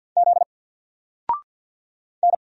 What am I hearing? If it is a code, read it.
Morse code, heard as a keyed tone, about 50 wpm, 700 Hz. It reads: Z N
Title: Single Letter-Number 50wpm